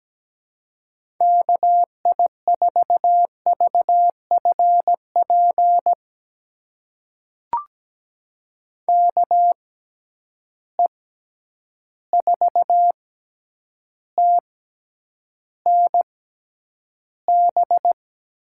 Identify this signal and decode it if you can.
Morse code, heard as a keyed tone, about 17 wpm, spaced as for 9 wpm, 700 Hz. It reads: KI4VFP KE4TNB
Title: USA Call Signs - 6 Characters - 6x Character Spacing - 17wpm